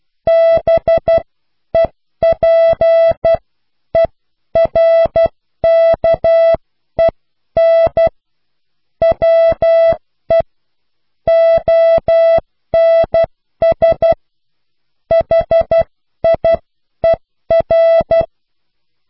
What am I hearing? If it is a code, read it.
Morse code, heard as a keyed tone, about 12 wpm, spaced as for 10 wpm, 660 Hz. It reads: BEPERKEN WE ONS HIER